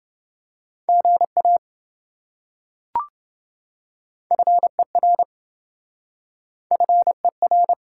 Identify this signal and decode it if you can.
Morse code, heard as a keyed tone, about 26 wpm, 700 Hz. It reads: GA FER FER